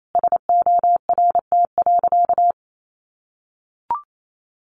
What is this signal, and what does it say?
Morse code, heard as a keyed tone, about 28 wpm, 700 Hz. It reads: SORT.
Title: Sentences from Top 1300 Words 28wpm